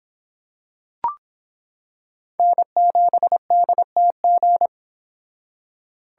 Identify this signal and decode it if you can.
Morse code, heard as a keyed tone, about 26 wpm, 700 Hz. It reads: N7DTG